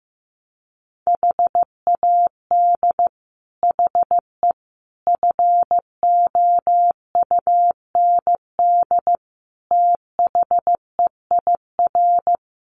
Morse code, heard as a keyed tone, about 15 wpm, 700 Hz. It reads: HAD HE FOUND THEIR
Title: Sentences from Top 1300 Words 15wpm